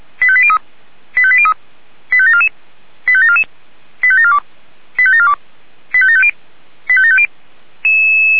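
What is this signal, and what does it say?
Five-tone selective call (ZVEI), heard as quick runs of five tones, tones of about 70 ms. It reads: 85803 85803 85744 85744 85742 85742 85750 85750
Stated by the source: Received on SDR as ZVEI five-tone selective call, address 89484.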